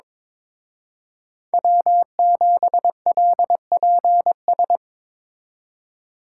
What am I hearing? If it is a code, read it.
Morse code, heard as a keyed tone, about 25 wpm, 700 Hz. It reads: W7LPS
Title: Speed Racing - USA 1x3 Call Signs 22wpm